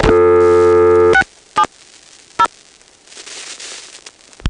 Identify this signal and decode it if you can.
DTMF, heard as telephone touch tones, tones of about 80 ms. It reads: 670
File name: phonetrips / PrinceWmCoVA5HQ